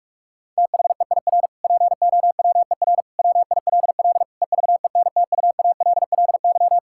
Morse code, heard as a keyed tone, about 46 wpm, 700 Hz. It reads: THEIR POWER WILL EVENTUALLY
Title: Sentences from Top 1000 Words - Review Part 6 - 45wpm